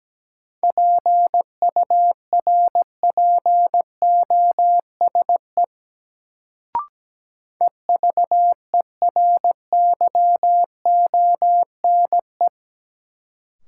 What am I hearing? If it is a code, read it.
Morse code, heard as a keyed tone, about 17 wpm, 700 Hz. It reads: PURPOSE EVERYONE
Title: Sentences from Top 800 Words - New Word Drill - 17wpm